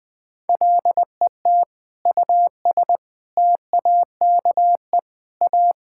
Morse code, heard as a keyed tone, about 20 wpm, 700 Hz. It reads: LET US TAKE A
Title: Sentences with 6-Word Phrases - Spoken Beginning - 20wpm